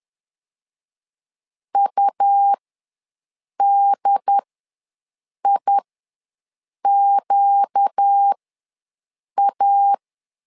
Morse code, heard as a keyed tone, about 10 wpm, spaced as for 6 wpm, 800 Hz. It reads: UDIQA
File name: LEZIONI / 4 Q ws 10wpm 2306